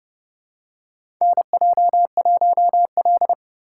Morse code, heard as a keyed tone, about 30 wpm, 700 Hz. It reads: NJ1L
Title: USA Call Signs - 4 Characters - 2x Character Spacing - 30wpm